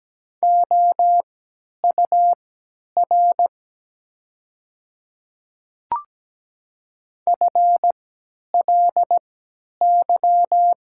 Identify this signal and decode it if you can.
Morse code, heard as a keyed tone, about 17 wpm, spaced as for 10 wpm, 700 Hz. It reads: OUR FLY
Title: 3x Character Spacing - Top 500 Words - 17wpm